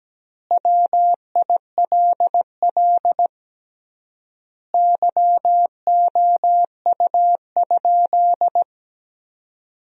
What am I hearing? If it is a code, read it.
Morse code, heard as a keyed tone, about 17 wpm, 700 Hz. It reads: WILL YOU?
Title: 3x Word-Spacing - Sentences from Top 400 Words - 17wpm